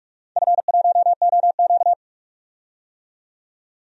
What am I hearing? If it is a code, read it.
Morse code, heard as a keyed tone, about 43 wpm, 700 Hz. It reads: F1OQ
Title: DXCC-100 Call Signs - 4 Characters - 45wpm